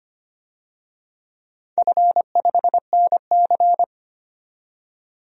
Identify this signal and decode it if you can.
Morse code, heard as a keyed tone, about 25 wpm, 700 Hz. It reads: F5NC
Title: DXCC-100 Call Signs - 4 Characters - 25wpm